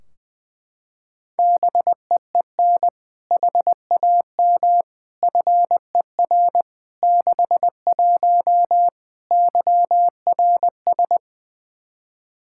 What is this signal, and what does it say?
Morse code, heard as a keyed tone, about 20 wpm, 700 Hz. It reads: BEEN HAM FER 61 YRS
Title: Practice QSO - Exchange 3 - More Info - 20wpm